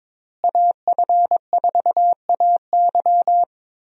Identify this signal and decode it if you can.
Morse code, heard as a keyed tone, about 22 wpm, 700 Hz. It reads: AF4AY
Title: USA Call Signs - 5 Characters - Repeated 3x - 22wpm